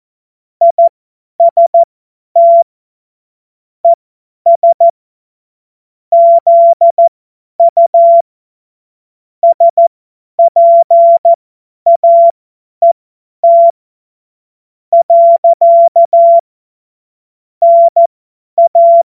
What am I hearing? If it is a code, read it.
Morse code, heard as a keyed tone, about 14 wpm, spaced as for 9 wpm, 680 Hz. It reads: IST ES ZU SPAET. NA
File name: Morsekurs / Witze / Witz 21.November 2023 50 F70